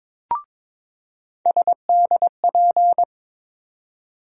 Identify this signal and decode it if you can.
Morse code, heard as a keyed tone, about 22 wpm, 700 Hz. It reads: SDP
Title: USA Call Sign Postfixes - 3 Letters 22wpm